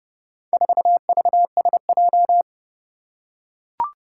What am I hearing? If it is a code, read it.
Morse code, heard as a keyed tone, about 28 wpm, 700 Hz. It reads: 4VSJ
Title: Four Challenging Letter-Number Combinations 30wpm